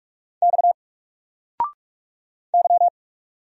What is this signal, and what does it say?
Morse code, heard as a keyed tone, about 46 wpm, 700 Hz. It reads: X Y
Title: Single Letter-Number 45wpm